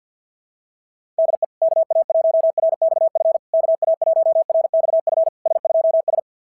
Morse code, heard as a keyed tone, about 46 wpm, 660 Hz. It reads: DE KA1RXF KA1RXF S2S
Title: SOTA QSO Practice 50wpm